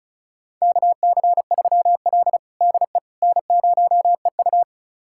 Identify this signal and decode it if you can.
Morse code, heard as a keyed tone, about 36 wpm, 700 Hz. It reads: KC3L DE N0EU
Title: USA Callsign DE Callsign - 4 Characters - 35wpm